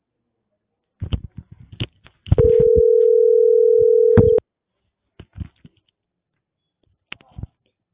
Ringback tone (North America), 440 + 480 Hz.